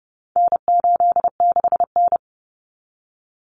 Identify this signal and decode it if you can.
Morse code, heard as a keyed tone, about 30 wpm, 700 Hz. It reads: N86N
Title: Common Cut Numbers - 4-Digit 30wpm